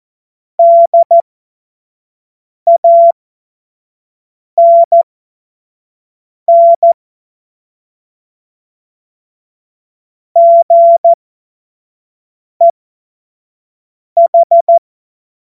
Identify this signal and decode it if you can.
Morse code, heard as a keyed tone, about 14 wpm, spaced as for 5 wpm, 680 Hz. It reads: DANN GEH
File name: Morsekurs / Witze / Witz 21. März 2023 25 F70